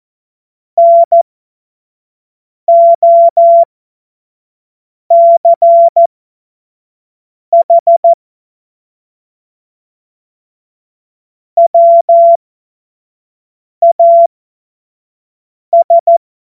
Morse code, heard as a keyed tone, about 14 wpm, spaced as for 5 wpm, 680 Hz. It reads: NOCH WAS
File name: Morsekurs / Witze / Witz 11. Januar 2022, 25 F70